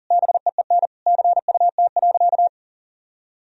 Morse code, heard as a keyed tone, about 40 wpm, 700 Hz. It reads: BEEN CUT.
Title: Sentences from Top 1000 Words - Review Part 1 - 40wpm